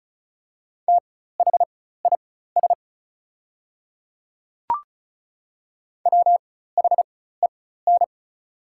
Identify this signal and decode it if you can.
Morse code, heard as a keyed tone, about 36 wpm, spaced as for 16 wpm, 700 Hz. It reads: THIS WHEN
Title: 4x Character Spacing - Top 200 Words - 35wpm